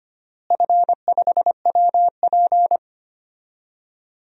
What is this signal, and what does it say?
Morse code, heard as a keyed tone, about 25 wpm, 700 Hz. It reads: F5WP
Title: DXCC-100 Call Signs - 4 Characters - Repeated 3x - 25wpm